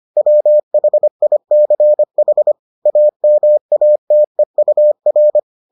Morse code, heard as a keyed tone, about 25 wpm, 600 Hz. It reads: WHICH AMATEUR